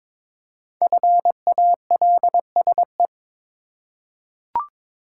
Morse code, heard as a keyed tone, about 22 wpm, 700 Hz. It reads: FALSE
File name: 5 Letter Words - First 3 Letters Given - 22wpm